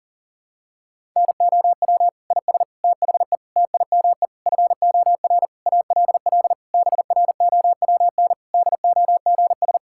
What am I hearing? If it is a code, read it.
Morse code, heard as a keyed tone, about 37 wpm, 700 Hz. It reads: NOW IS THE TIME FOR ALL BROWN DOGS
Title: Pangram Challenge v2 - 40wpm